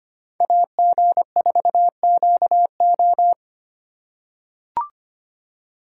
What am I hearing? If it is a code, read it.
Morse code, heard as a keyed tone, about 25 wpm, 700 Hz. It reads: AG4QO
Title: USA Call Signs - 2x Character Spacing - 25wpm